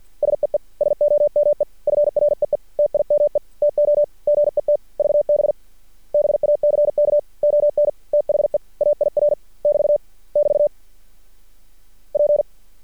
Morse code, heard as a keyed tone, about 50 wpm, 590 Hz. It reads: SEE SOME FREE TIME TO GET VB BACK ON THE AIR <BT> <BT> P
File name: archived / CRSnet